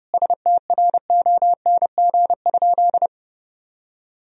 Morse code, heard as a keyed tone, about 30 wpm, 700 Hz. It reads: STRONG?
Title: Sentences from Top 1000 Words - Review Part 3 - 30wpm